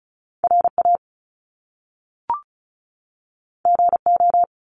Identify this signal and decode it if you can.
Morse code, heard as a keyed tone, about 36 wpm, 700 Hz. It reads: RA GO
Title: Top Two Letter Combinations in Words 35wpm